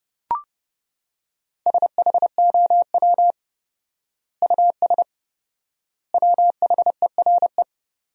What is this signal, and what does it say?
Morse code, heard as a keyed tone, about 30 wpm, 700 Hz. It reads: SHOW US WHERE
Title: 4x Word-Spacing - Sentences from Top 200 Words - 30wpm